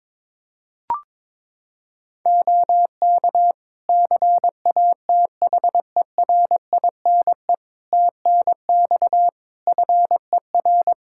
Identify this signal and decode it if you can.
Morse code, heard as a keyed tone, about 22 wpm, 700 Hz. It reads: OK CATHERINE TNX FER
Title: Practice QSO - Exchange 4 - Ending - 22wpm